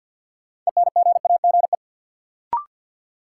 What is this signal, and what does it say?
Morse code, heard as a keyed tone, about 47 wpm, 700 Hz. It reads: ENGAGE
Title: Sentences from Top 1500 Words - New Word Drill - 50wpm